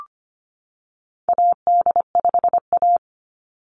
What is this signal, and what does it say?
Morse code, heard as a keyed tone, about 25 wpm, 700 Hz. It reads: AD5A
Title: NA SOTA Activator Callsigns 25wpm